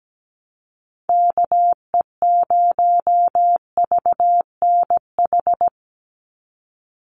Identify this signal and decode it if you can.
Morse code, heard as a keyed tone, about 17 wpm, 700 Hz. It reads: KE0VNH
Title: USA Call Signs 17wpm